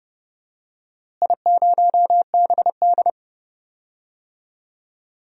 Morse code, heard as a keyed tone, about 30 wpm, 700 Hz. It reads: I0BD
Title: DXCC-100 Call Signs - 4 Characters - Repeated 2x - 30wpm